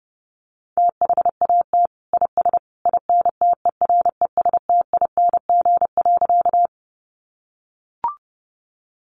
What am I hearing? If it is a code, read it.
Morse code, heard as a keyed tone, about 28 wpm, 700 Hz. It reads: THAT IS INTERESTING.
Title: Sentences from Top 1000 Words 30wpm